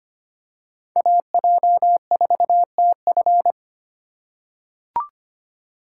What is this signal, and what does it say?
Morse code, heard as a keyed tone, about 25 wpm, 700 Hz. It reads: AJ4TF
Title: K1USN SST QSO Practice - 25wpm